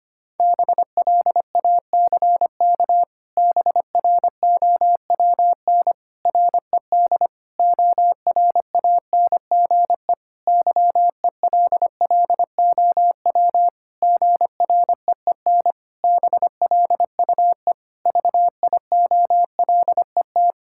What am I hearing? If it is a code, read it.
Morse code, heard as a keyed tone, about 25 wpm, 700 Hz. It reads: BLACK BROWN RED ORANGE YELLOW GREEN BLUE VIOLET
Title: Pangram Challenge v2 - 25wpm